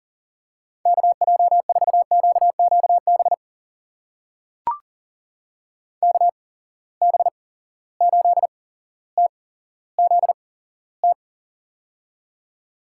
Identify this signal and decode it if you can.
Morse code, heard as a keyed tone, about 40 wpm, spaced as for 17 wpm, 700 Hz. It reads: KJ4QQB KB8TZT
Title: USA Call Signs - 6 Characters - 8x Character Spacing - 40wpm